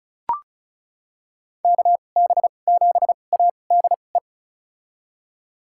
Morse code, heard as a keyed tone, about 34 wpm, spaced as for 25 wpm, 700 Hz. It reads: KB7ADE
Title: USA Call Signs - 2x Character Spacing - 35wpm